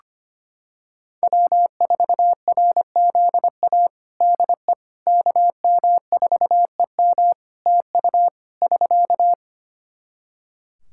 Morse code, heard as a keyed tone, about 25 wpm, 700 Hz. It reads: W4RZA DE KM4EM TU <SK>
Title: Practice QSO - Exchange 4 - Ending - 25wpm